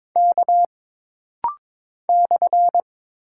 Morse code, heard as a keyed tone, about 22 wpm, 700 Hz. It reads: K /
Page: This rapid-fire timing is geared to push your ICR (Instant Character Recognition) skills.